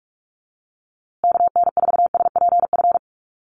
Morse code, heard as a keyed tone, about 44 wpm, 700 Hz. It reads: KN4SPF
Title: USA Call Signs Repeated 3x 45wpm